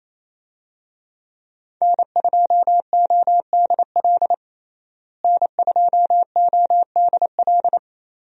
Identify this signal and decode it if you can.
Morse code, heard as a keyed tone, about 28 wpm, 700 Hz. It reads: N2ODL N2ODL
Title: USA Call Signs - 5 Characters - Repeated 2x - 28wpm